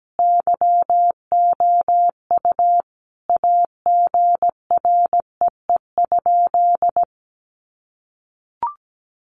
Morse code, heard as a keyed tone, about 17 wpm, 700 Hz. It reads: YOU AGREE?
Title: Sentences from Top 1100 Words 17wpm